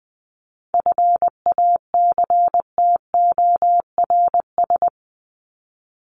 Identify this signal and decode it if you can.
Morse code, heard as a keyed tone, about 20 wpm, 700 Hz. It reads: FACTORS
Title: ES:S Suffix - Top 100 Words with Base Words 20wpm